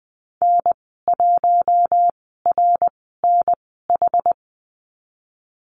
Morse code, heard as a keyed tone, about 19 wpm, spaced as for 14 wpm, 700 Hz. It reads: N1RNH
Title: 069 Lesson 12g - Letter d - Call Signs 20wpm